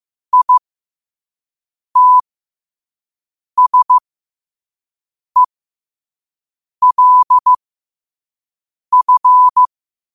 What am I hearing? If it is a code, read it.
Morse code, heard as a keyed tone, about 15 wpm, spaced as for 5 wpm, 1000 Hz. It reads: ITSELF
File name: qotdfiles / 5wpm